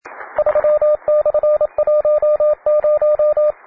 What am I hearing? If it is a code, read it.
Morse code, heard as a keyed tone, about 27 wpm, 600 Hz, with radio noise. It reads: 3/10